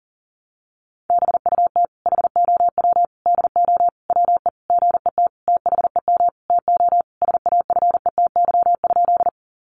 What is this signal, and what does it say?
Morse code, heard as a keyed tone, about 40 wpm, 700 Hz. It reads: BUT HOW DO WE GET THEM TO SAFETY?
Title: Sentences from Top 700 Words 40wpm